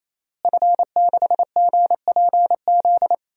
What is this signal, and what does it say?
Morse code, heard as a keyed tone, about 28 wpm, 700 Hz. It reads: F6GPZ
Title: DXCC-100 Call Signs - 5 Characters - Repeated 3x - 28wpm